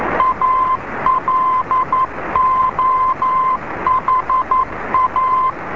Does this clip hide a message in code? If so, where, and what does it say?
Yes: Morse code from 0.2 to 5.5 s, heard as a keyed tone, about 11 wpm, 1030 Hz, with noise. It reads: ALOHA